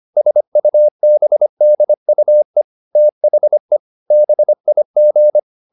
Morse code, heard as a keyed tone, about 25 wpm, 600 Hz. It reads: SUBDUE THE BIG
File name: the old man and the sea part 1 25wpm